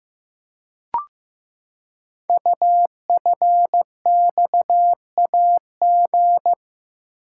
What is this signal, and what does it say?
Morse code, heard as a keyed tone, about 15 wpm, 700 Hz. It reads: UFXAG
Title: Code Groups - 5 Letters - 15wpm